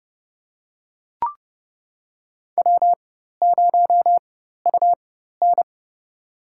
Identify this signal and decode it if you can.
Morse code, heard as a keyed tone, about 31 wpm, spaced as for 14 wpm, 700 Hz. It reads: W0UN
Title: USA Call Signs - 4 Characters - 4x Character Spacing - 30wpm